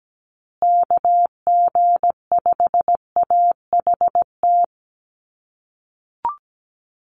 Morse code, heard as a keyed tone, about 17 wpm, 700 Hz. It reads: KG5AHT
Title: USA Call Signs Repeated 3x 17wpm